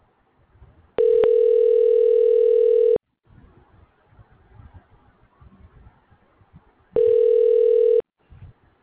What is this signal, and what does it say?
Ringback tone (North America), 440 + 480 Hz.